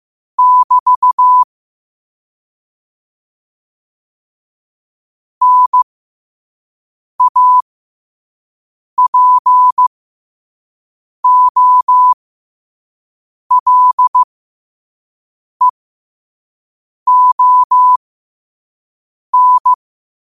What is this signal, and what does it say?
Morse code, heard as a keyed tone, about 15 wpm, spaced as for 5 wpm, 1000 Hz. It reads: <BT> NAPOLEON
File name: qotdfiles / 5wpm